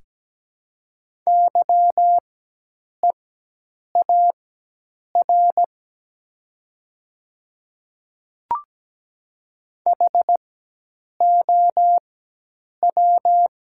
Morse code, heard as a keyed tone, about 17 wpm, spaced as for 8 wpm, 700 Hz. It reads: YEAR HOW
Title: 4x Character Spacing - Top 200 Words - 17wpm